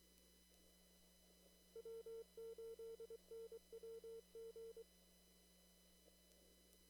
Morse code, heard as a keyed tone, about 23 wpm, 450 Hz. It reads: W8NWG
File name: CW / kt5j wpx cw-op-k5pi